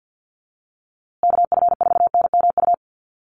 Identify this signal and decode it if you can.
Morse code, heard as a keyed tone, about 50 wpm, 700 Hz. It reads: KF4NMU
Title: USA Call Signs Repeated 3x 50wpm